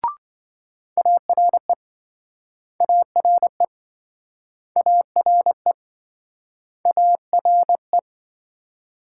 Morse code, heard as a keyed tone, about 27 wpm, 700 Hz. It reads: ARE ARE ARE ARE